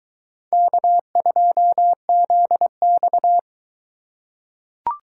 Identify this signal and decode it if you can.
Morse code, heard as a keyed tone, about 20 wpm, 700 Hz. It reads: K2ZX